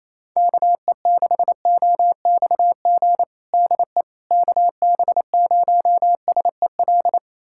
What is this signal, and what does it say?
Morse code, heard as a keyed tone, about 28 wpm, 700 Hz. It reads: KE6OXG DE KB0SEL
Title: USA Callsign DE Callsign - 6 Characters - 28wpm